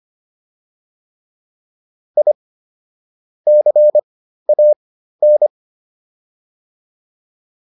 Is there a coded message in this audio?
Yes: Morse code from 2.2 to 5.5 s, heard as a keyed tone, about 25 wpm, spaced as for 13 wpm, 600 Hz. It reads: I CAN